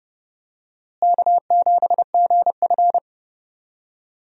Morse code, heard as a keyed tone, about 31 wpm, 700 Hz. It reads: K7GF
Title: USA Call Signs - 4 Characters - Repeated 3x - 30wpm